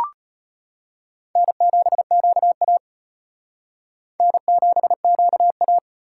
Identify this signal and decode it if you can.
Morse code, heard as a keyed tone, about 32 wpm, 700 Hz. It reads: N7QA N7QA